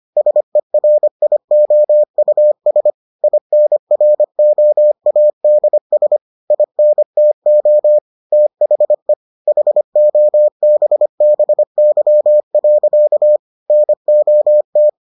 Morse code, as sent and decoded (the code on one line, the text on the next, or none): ... .-. .. --- ..- ... / .. -. .-. --- .- -.. ... / .. -. - --- / - .... / .... --- -... -... -.-- .-.-.- / -. --- -
SERIOUS INROADS INTO THE HOBBY. NOT